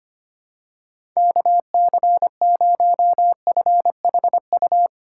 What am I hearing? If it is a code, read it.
Morse code, heard as a keyed tone, about 25 wpm, 700 Hz. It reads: KC0FHU
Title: USA Call Signs - 6 Characters - 3x Character Spacing - 25wpm